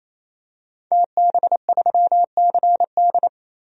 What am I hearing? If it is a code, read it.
Morse code, heard as a keyed tone, about 28 wpm, 700 Hz. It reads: TB3CD
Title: DXCC-100 Call Signs - 5 Characters - 28wpm